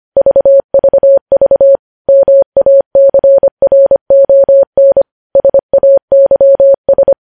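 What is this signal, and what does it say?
Morse code, heard as a keyed tone, about 25 wpm, 550 Hz. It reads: VVV MACRON SAYS
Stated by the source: Morse code news headlines at 25 wpm.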